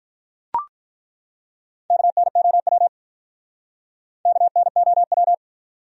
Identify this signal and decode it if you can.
Morse code, heard as a keyed tone, about 43 wpm, 700 Hz. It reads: KNOW KNOW
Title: Speed-Racing Top 100 Words 35wpm